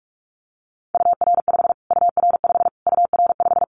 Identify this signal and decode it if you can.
Morse code, heard as a keyed tone, about 44 wpm, 700 Hz. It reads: UR5 UR5 UR5
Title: Callsign Prefixes for Top 25 DXCC - Repeated 3x - 45wpm